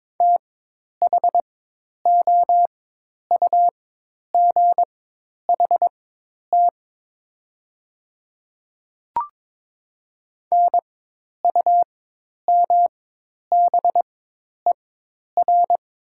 Morse code, heard as a keyed tone, about 22 wpm, spaced as for 10 wpm, 700 Hz. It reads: THOUGHT NUMBER
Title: 4x Character Spacing - Top 500 Words - 22wpm